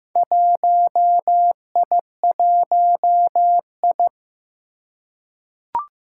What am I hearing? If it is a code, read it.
Morse code, heard as a keyed tone, about 15 wpm, 700 Hz. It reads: 1I1I
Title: Four Challenging Letter-Number Combinations 15wpm